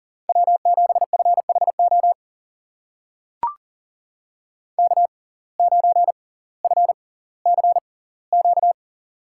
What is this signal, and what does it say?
Morse code, heard as a keyed tone, about 40 wpm, spaced as for 20 wpm, 700 Hz. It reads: W7FHO K9FCQ
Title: USA Call Signs - 5 Characters - 6x Character Spacing - 40wpm